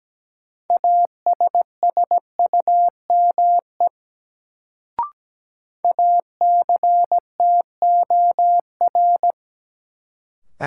Morse code, heard as a keyed tone, about 17 wpm, 700 Hz. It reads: ASSUME ACTOR
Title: Sentences from Top 1500 Words - New Word Drill - 17wpm